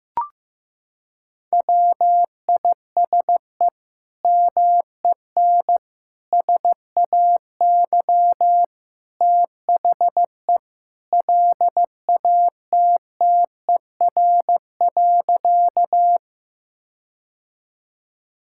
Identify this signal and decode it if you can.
Morse code, heard as a keyed tone, about 15 wpm, 700 Hz. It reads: WISE MEN SAY THE LATTER.
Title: Sentences from Top 1600 Words 15wpm